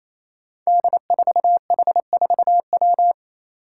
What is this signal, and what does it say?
Morse code, heard as a keyed tone, about 28 wpm, 700 Hz. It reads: D4H4W
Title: Five Challenging Letter-Number Combinations 28wpm